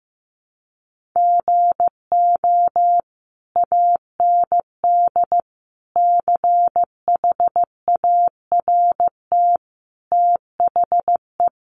Morse code, heard as a keyed tone, about 15 wpm, 700 Hz. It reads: GO AND CHART THE
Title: Sentences from Top 1300 Words 15wpm